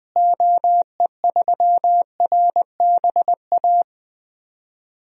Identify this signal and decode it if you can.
Morse code, heard as a keyed tone, about 21 wpm, 700 Hz. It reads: OE3RBA
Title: DXCC-100 Call Signs - 6 Characters - Repeated 3x - 20wpm